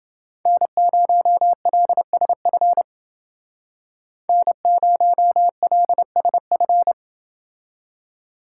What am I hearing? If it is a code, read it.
Morse code, heard as a keyed tone, about 29 wpm, 700 Hz. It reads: N0LSF N0LSF